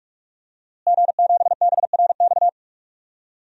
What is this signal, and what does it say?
Morse code, heard as a keyed tone, about 44 wpm, 700 Hz. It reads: G7BRX
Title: DXCC-100 Call Signs - 5 Characters - Repeated 2x - 45wpm